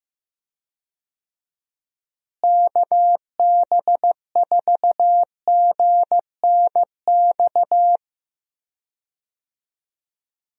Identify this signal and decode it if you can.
Morse code, heard as a keyed tone, about 15 wpm, 700 Hz. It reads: KB4GNX
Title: USA Call Signs - 6 Characters - Repeated 2x - 15wpm